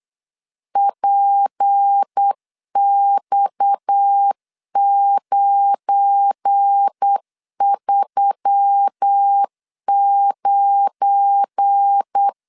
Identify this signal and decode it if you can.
Morse code, heard as a keyed tone, about 8 wpm, 800 Hz. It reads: PX939
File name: LEZIONI / 7 GF9Z / 7 GF9 8wpm 3545